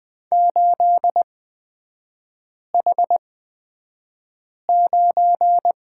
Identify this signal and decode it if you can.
Morse code, heard as a keyed tone, about 20 wpm, spaced as for 5 wpm, 700 Hz. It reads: 8H9
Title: 224 Lesson 36e - BT - Three Characters 20wpm